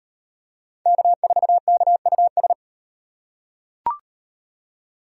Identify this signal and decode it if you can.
Morse code, heard as a keyed tone, about 38 wpm, 700 Hz. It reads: K4KUS